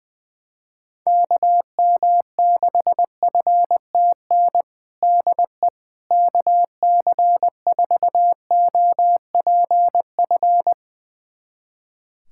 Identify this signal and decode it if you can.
Morse code, heard as a keyed tone, about 21 wpm, 700 Hz. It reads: KM6FTN DE KC4OPF
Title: USA Callsign DE Callsign 20wpm